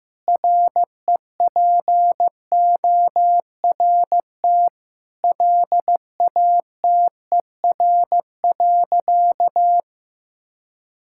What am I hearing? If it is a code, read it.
Morse code, heard as a keyed tone, about 15 wpm, 700 Hz. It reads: REPORT LATER.
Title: Sentences from Top 1000 Words - Review Part 6 - 15wpm